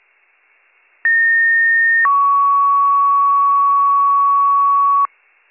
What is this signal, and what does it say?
Two-tone page, heard as a one-second tone, then a three-second tone, about 1850 then 1120 Hz. It is heard from 1.0 to 5.1 s.